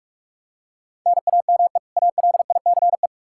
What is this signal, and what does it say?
Morse code, heard as a keyed tone, about 43 wpm, 700 Hz. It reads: NAME ALICE
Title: Practice QSO - Exchange 1 - Beginning - 45wpm